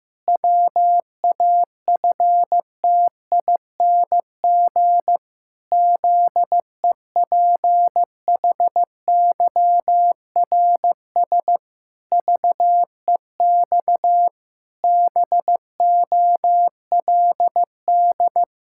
Morse code, heard as a keyed tone, about 15 wpm, 700 Hz. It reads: WAFTING ZEPHYRS VEX BOLD
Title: Pangram Challenge v2 - 15wpm